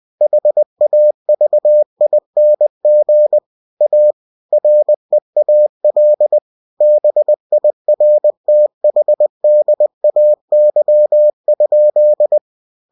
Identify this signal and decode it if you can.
Morse code, heard as a keyed tone, about 20 wpm, 600 Hz. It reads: HAVING A REAL BIRTHDAY?
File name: cw-books / winnie-the-pooh / winnie-20wpm-page-0054